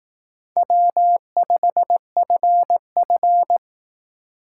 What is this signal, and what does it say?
Morse code, heard as a keyed tone, about 18 wpm, 700 Hz. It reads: W5FF
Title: Speed Racing - USA 1x2 Call Signs 15wpm